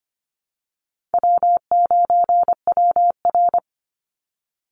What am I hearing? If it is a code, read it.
Morse code, heard as a keyed tone, about 25 wpm, 700 Hz. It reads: W9WR
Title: Easy Call Signs Repeated 3x - 25wpm